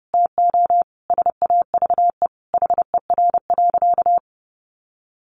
Morse code, heard as a keyed tone, about 30 wpm, 700 Hz. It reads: TO SAVE HER.
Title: Sentences from Top 1100 Words 30wpm